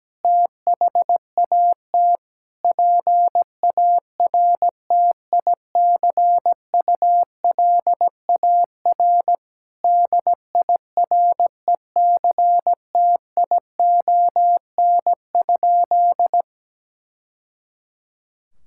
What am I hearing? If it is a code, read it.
Morse code, heard as a keyed tone, about 17 wpm, 700 Hz. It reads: THAT PARTICULAR DIRECTION?
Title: Sentences from Top 1000 Words - Review Part 5 - 17wpm